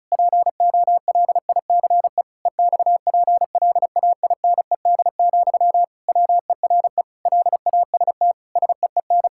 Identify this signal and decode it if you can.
Morse code, heard as a keyed tone, about 36 wpm, 700 Hz. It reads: POLICE EXPLAINED, WERE LAST SEEN
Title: Pangram Challenge v2 - 35wpm